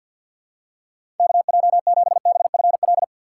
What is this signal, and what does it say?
Morse code, heard as a keyed tone, about 53 wpm, 700 Hz. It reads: KJ7BFL
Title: USA Call Signs - 6 Characters - 6x Character Spacing - 50wpm